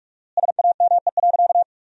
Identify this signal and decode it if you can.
Morse code, heard as a keyed tone, about 44 wpm, 700 Hz. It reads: SAME.
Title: 4x Word-Spacing - Sentences from Top 300 Words - 45wpm